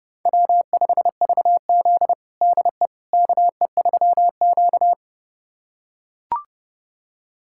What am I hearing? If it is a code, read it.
Morse code, heard as a keyed tone, about 30 wpm, 700 Hz. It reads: W5VZ DE KE3Q
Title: USA Callsign DE Callsign - 4 Characters - 30wpm